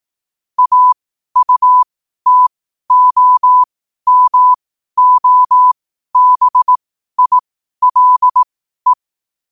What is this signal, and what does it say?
Morse code, heard as a keyed tone, about 18 wpm, spaced as for 12 wpm, 1000 Hz. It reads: AUTOMOBILE
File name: qotdfiles / 12wpm